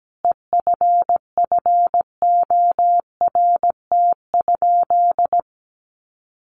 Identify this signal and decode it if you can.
Morse code, heard as a keyed tone, about 17 wpm, 700 Hz. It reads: EFFORT?
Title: Sentences from Top 1400 Words 17wpm